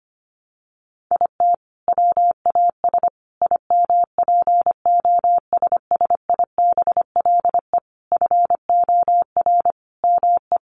Morse code, heard as a keyed tone, about 25 wpm, 700 Hz. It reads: IT WAS IMPOSSIBLE FOR ME
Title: Sentences from Top 1100 Words 25wpm